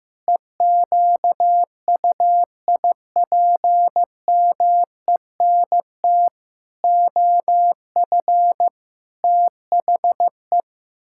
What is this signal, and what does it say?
Morse code, heard as a keyed tone, about 15 wpm, 700 Hz. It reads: EQUIPMENT OF THE
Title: Pangram Challenge v2 - 15wpm